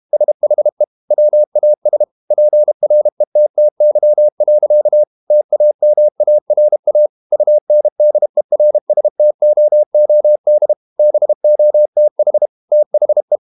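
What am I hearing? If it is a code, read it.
Morse code, heard as a keyed tone, about 32 wpm, 600 Hz. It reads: SHE WAS PRETTY. TAMARA UNDERSTOOD BOTH THE
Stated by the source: Morse code practice at 32 wpm from CW med Gnister, reading Ken Follett's novel Never in English.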